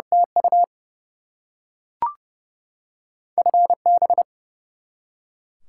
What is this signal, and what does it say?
Morse code, heard as a keyed tone, about 32 wpm, 700 Hz. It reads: TU FB
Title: Top 90 QSO Elements 30wpm